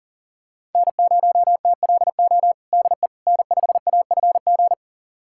Morse code, heard as a keyed tone, about 40 wpm, 700 Hz. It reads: N0TLO DE N5AFG
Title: USA Callsign DE Callsign - 5 Characters - 40wpm